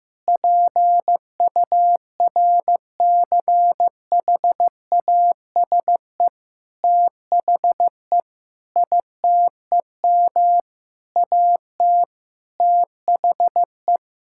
Morse code, heard as a keyed tone, about 15 wpm, 700 Hz. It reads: PURCHASE THE ITEM AT THE
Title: Sentences from Top 1000 Words - Review Part 6 - 15wpm